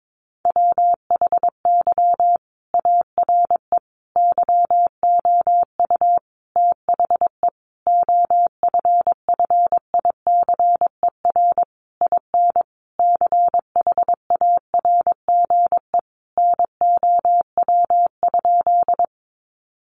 Morse code, heard as a keyed tone, about 22 wpm, 700 Hz. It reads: WHY ARE YOU THE OFFICER IN CHARGE NOW?
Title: Sentences from Top 1200 Words 22wpm